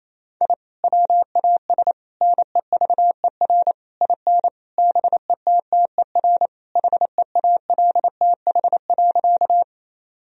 Morse code, heard as a keyed tone, about 28 wpm, 700 Hz. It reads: I WAS NEVER IN BETTER HEALTH.
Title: Sentences from Top 1000 Words - Review Part 3 - 28wpm